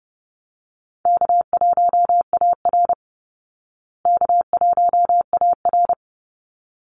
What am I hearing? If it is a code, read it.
Morse code, heard as a keyed tone, about 30 wpm, 700 Hz. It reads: K1AR K1AR
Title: Easy Call Signs Repeated 2x - 30wpm